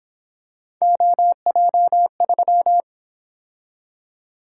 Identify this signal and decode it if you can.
Morse code, heard as a keyed tone, about 27 wpm, 700 Hz. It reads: OJ3
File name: Speed-Racing - Three Letter-Number Combinations - 22wpm